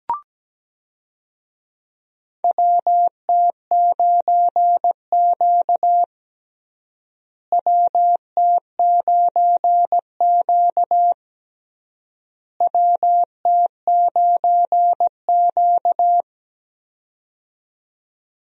Morse code, heard as a keyed tone, about 17 wpm, 700 Hz. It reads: WT9Q WT9Q WT9Q
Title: USA Call Signs - 4 Characters - Repeated 3x - 17wpm